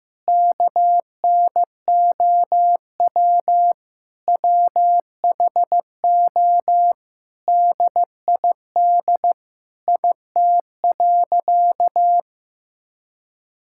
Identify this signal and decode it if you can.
Morse code, heard as a keyed tone, about 15 wpm, 700 Hz. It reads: KNOW WHO DID IT.
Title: Sentences from Top 1000 Words - Review Part 6 - 15wpm